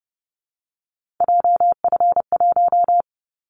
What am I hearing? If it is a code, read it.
Morse code, heard as a keyed tone, about 30 wpm, 700 Hz. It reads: JF1
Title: Callsign Prefixes for Top 25 DXCC - 30wpm